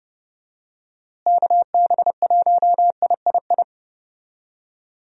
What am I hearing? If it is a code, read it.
Morse code, heard as a keyed tone, about 30 wpm, 700 Hz. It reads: KB1III
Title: USA Call Signs - 6 Characters - Repeated 3x - 30wpm